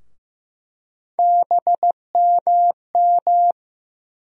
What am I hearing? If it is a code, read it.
Morse code, heard as a keyed tone, about 15 wpm, 700 Hz. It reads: BMM
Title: USA Call Sign Postfixes - 3 Letters 15wpm